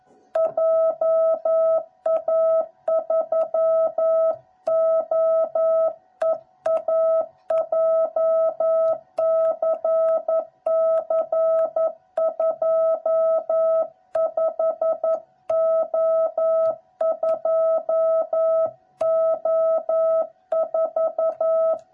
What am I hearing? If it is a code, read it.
Morse code, heard as a keyed tone, about 11 wpm, 660 Hz. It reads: JA3OEA JCC25O2O4
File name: jh8cbh / 2021cw / cw06b